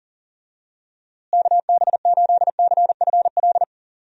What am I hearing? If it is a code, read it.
Morse code, heard as a keyed tone, about 40 wpm, 700 Hz. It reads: KB8CFL
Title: USA Call Signs - 6 Characters - Repeated 2x - 40wpm